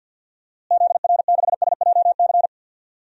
Morse code, heard as a keyed tone, about 52 wpm, 700 Hz. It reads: ZR6SJC
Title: DXCC-100 Call Signs - 6 Characters - Repeated 3x - 50wpm